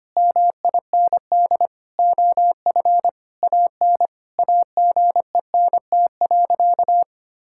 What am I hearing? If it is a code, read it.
Morse code, heard as a keyed tone, about 25 wpm, 700 Hz. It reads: MIND OF AN AGENT.
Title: Sentences from Top 1600 Words 25wpm